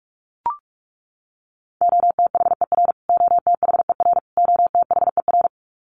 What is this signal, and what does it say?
Morse code, heard as a keyed tone, about 44 wpm, 700 Hz. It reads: OTHER OTHER OTHER
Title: Top 200 Words Repeated 45wpm